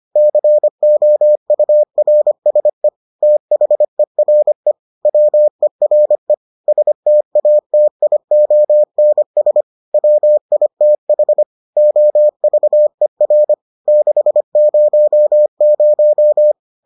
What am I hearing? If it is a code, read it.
Morse code, heard as a keyed tone, about 25 wpm, 600 Hz. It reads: COURSE THERE WERE STATIONS WITH OVER 600